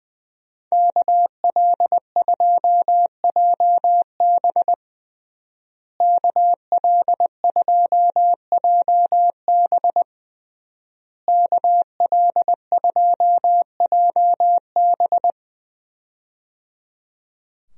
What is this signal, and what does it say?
Morse code, heard as a keyed tone, about 20 wpm, 700 Hz. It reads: KL2JB KL2JB KL2JB
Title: USA Call Signs - 5 Characters - Repeated 3x - 20wpm